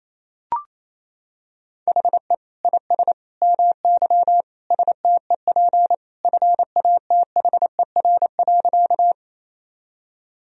Morse code, heard as a keyed tone, about 28 wpm, 700 Hz. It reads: HE IS MY STEP FATHER.
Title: Sentences from Top 1000 Words - Review Part 2 - 28wpm